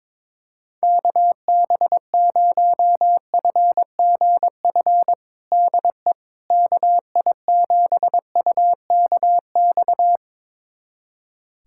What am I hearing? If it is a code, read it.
Morse code, heard as a keyed tone, about 21 wpm, 700 Hz. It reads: KB0FGF DE KI7UKX
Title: USA Callsign DE Callsign - 6 Characters - 22wpm